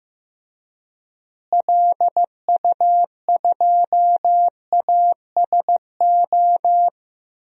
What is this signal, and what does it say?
Morse code, heard as a keyed tone, about 15 wpm, 700 Hz. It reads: LU2ASO
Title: DXCC-100 Call Signs - 6 Characters - Repeated 3x - 15wpm